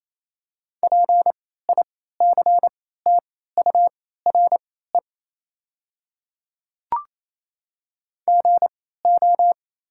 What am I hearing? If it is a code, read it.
Morse code, heard as a keyed tone, about 28 wpm, spaced as for 16 wpm, 700 Hz. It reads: PICTURE GO
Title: 3x Character Spacing - Top 200 Words - 28wpm